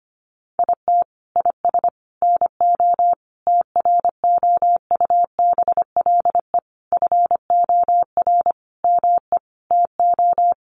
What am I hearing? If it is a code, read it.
Morse code, heard as a keyed tone, about 25 wpm, 700 Hz. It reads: IT IS NO TROUBLE FOR ME TO
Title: Sentences from Top 1200 Words 25wpm